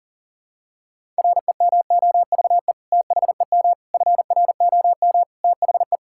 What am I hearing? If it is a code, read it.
Morse code, heard as a keyed tone, about 40 wpm, 700 Hz. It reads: REMOVE THEM FROM THE
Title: Sentences from Top 1000 Words - Review Part 5 - 40wpm